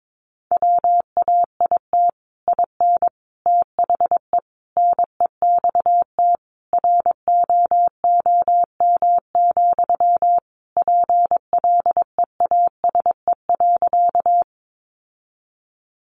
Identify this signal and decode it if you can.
Morse code, heard as a keyed tone, about 22 wpm, 700 Hz. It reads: WAIT IN THE NEXT ROOM, PLEASE.
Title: Sentences from Top 1500 Words 22wpm